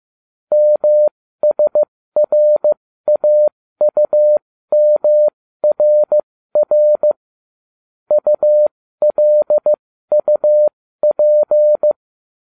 Morse code, heard as a keyed tone, about 15 wpm, spaced as for 13 wpm, 600 Hz. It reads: MSRAUMRR ULUP